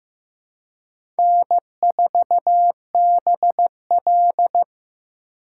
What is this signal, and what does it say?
Morse code, heard as a keyed tone, about 15 wpm, 700 Hz. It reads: N4BL
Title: USA Call Signs - 4 Characters - 3x Character Spacing - 15wpm